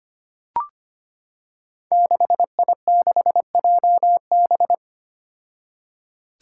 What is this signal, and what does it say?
Morse code, heard as a keyed tone, about 25 wpm, 700 Hz. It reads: 6I6JB
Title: Five Challenging Letter-Number Combinations 25wpm